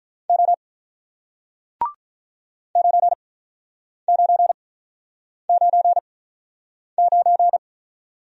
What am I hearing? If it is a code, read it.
Morse code, heard as a keyed tone, about 48 wpm, spaced as for 9 wpm, 700 Hz. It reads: O 9999